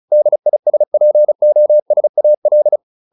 Morse code, heard as a keyed tone, about 34 wpm, 600 Hz. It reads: DISPOSAL